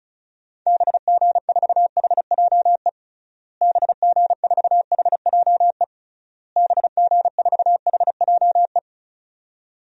Morse code, heard as a keyed tone, about 34 wpm, 700 Hz. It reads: BG4HJE BG4HJE BG4HJE
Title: DXCC-100 Call Signs - 6 Characters - Repeated 3x - 35wpm